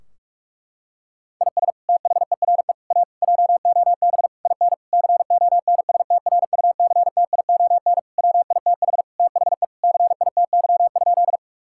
Morse code, heard as a keyed tone, about 46 wpm, 700 Hz. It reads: IS THERE A JOB IN CONSTRUCTION WITH THE CITY?
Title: Sentences from Top 1600 Words 45wpm